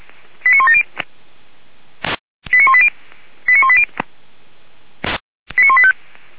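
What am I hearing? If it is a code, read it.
Five-tone selective call (ZVEI), heard as quick runs of five tones, tones of about 70 ms. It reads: 89170 89180 89180 89185
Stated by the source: ZVEI five-tone selective call 89138, received by SDR, from the feuerwehr-orlamuende fire brigade audio server.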